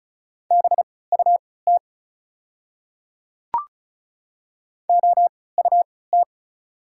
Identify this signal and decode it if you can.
Morse code, heard as a keyed tone, about 36 wpm, spaced as for 20 wpm, 690 Hz. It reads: BUT OUT
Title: 3x Character Spacing - Top 500 Words - 35wpm